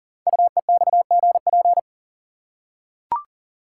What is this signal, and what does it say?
Morse code, heard as a keyed tone, about 42 wpm, 690 Hz. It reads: UEXGP
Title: Code Groups - 5 Letters - 40wpm